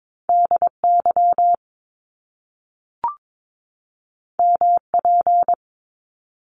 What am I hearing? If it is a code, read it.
Morse code, heard as a keyed tone, about 22 wpm, 700 Hz. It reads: DY MP
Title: Top Two Letter Combinations in Words 22wpm